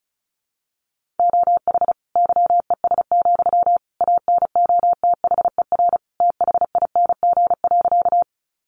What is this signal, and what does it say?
Morse code, heard as a keyed tone, about 34 wpm, 700 Hz. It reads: OH YES, ANOTHER THING.
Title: Sentences from Top 600 Words 35wpm